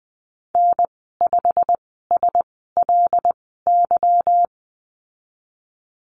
Morse code, heard as a keyed tone, about 20 wpm, spaced as for 15 wpm, 700 Hz. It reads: N5SLY
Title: 161 Lesson 26g - Letter g - Call Signs 20wpm